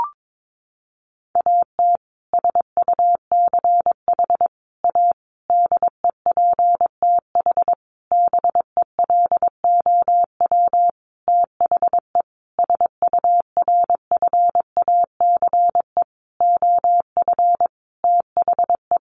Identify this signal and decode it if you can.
Morse code, heard as a keyed tone, about 22 wpm, 700 Hz. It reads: AT SUCH A DEPTH BELOW THE SURFACE OF THE
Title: Sentences from Top 1200 Words 22wpm